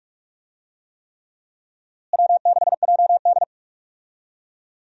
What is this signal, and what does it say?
Morse code, heard as a keyed tone, about 42 wpm, 700 Hz. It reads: W6JD
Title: USA Call Signs - 4 Characters - Repeated 3x - 45wpm